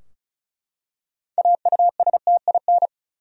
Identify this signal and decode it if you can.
Morse code, heard as a keyed tone, about 36 wpm, 700 Hz. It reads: AUSTIN
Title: USA Common Names - 35wpm